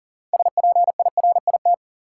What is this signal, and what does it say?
Morse code, heard as a keyed tone, about 40 wpm, 700 Hz. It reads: SPIRIT
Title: Sentences from Top 1400 Words - New Word Drill - 40wpm